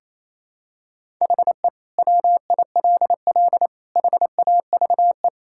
Morse code, heard as a keyed tone, about 28 wpm, 700 Hz. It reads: HE WILL HAVE
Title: Sentences from Top 1000 Words - Review Part 6 - 28wpm